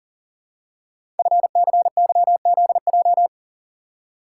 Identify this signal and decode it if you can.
Morse code, heard as a keyed tone, about 40 wpm, 700 Hz. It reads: FCYZJ
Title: Code Groups - 5 Letters - 40wpm